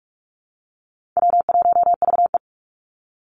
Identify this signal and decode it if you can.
Morse code, heard as a keyed tone, about 44 wpm, 700 Hz. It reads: W1VE
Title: Easy Call Signs - 45wpm